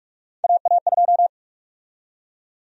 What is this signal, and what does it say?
Morse code, heard as a keyed tone, about 41 wpm, 700 Hz. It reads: AA2
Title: USA Call Sign Prefixes - 45wpm